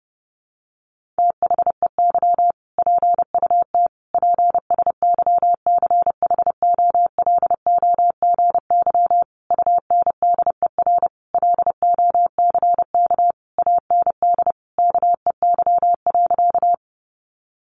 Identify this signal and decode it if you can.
Morse code, heard as a keyed tone, about 28 wpm, 700 Hz. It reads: THEY PUT PSYCHOLOGY UNDER LOCK AND KEY.
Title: Sentences from Top 1400 Words 30wpm